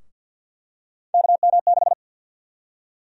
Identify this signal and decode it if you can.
Morse code, heard as a keyed tone, about 46 wpm, 700 Hz. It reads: KM6
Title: USA Call Sign Prefixes - Repeated 2x - 50wpm